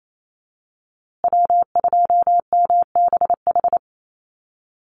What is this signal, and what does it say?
Morse code, heard as a keyed tone, about 28 wpm, 700 Hz. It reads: W2MBH
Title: USA Call Signs Repeated 2x 28wpm